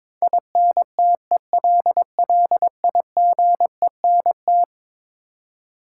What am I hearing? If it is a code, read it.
Morse code, heard as a keyed tone, about 22 wpm, 700 Hz. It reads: INTELLIGENT
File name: practice / 22wpm / Eleven Letter Words - 22wpm